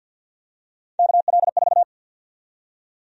Morse code, heard as a keyed tone, about 52 wpm, 700 Hz. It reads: KL4
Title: USA Call Sign Prefixes - Repeated 2x - 50wpm